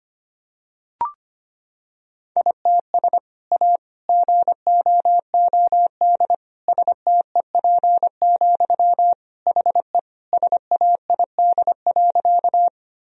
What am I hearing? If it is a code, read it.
Morse code, heard as a keyed tone, about 25 wpm, 700 Hz. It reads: ITS A GOOD STEP, HE SAID.
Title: Sentences from Top 1000 Words - Review Part 2 - 25wpm